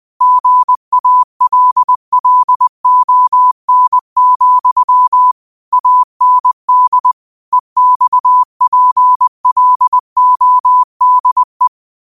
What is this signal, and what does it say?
Morse code, heard as a keyed tone, about 20 wpm, 1000 Hz. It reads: GALLON, AND EXPLODE
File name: qotdfiles / 20wpm